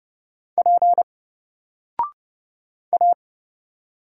Morse code, heard as a keyed tone, about 30 wpm, 700 Hz. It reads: P A
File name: Single-Letter-Number-30wpm